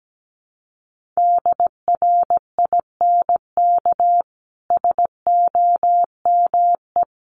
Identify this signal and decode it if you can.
Morse code, heard as a keyed tone, about 17 wpm, 700 Hz. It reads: DRINK SOME
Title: Sentences from Top 1500 Words 17wpm